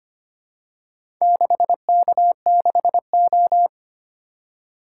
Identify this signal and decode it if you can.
Morse code, heard as a keyed tone, about 25 wpm, 700 Hz. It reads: 6K6O
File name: practice / 25wpm / Four Letter-Number Combinations - 25wpm